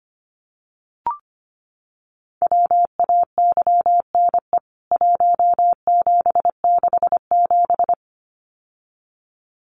Morse code, heard as a keyed tone, about 25 wpm, 700 Hz. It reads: WAYNE 1767
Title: CWT Exchange 25wpm